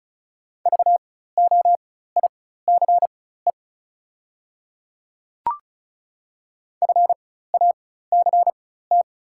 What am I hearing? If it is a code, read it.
Morse code, heard as a keyed tone, about 36 wpm, spaced as for 16 wpm, 700 Hz. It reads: VOICE FACT
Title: 4x Character Spacing - Top 500 Words - 35wpm